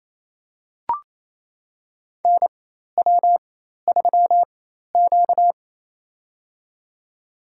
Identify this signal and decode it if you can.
Morse code, heard as a keyed tone, about 28 wpm, spaced as for 13 wpm, 700 Hz. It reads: NW3Q
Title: USA Call Signs - 4 Characters - 4x Character Spacing - 28wpm